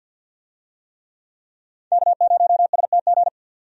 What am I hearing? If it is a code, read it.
Morse code, heard as a keyed tone, about 49 wpm, 700 Hz. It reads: K0STG
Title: USA Call Signs - 5 Characters - 50wpm